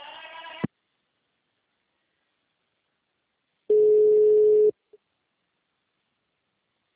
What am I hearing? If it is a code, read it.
Ringback tone (Europe, 425 Hz).
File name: out- -2010- - - .87059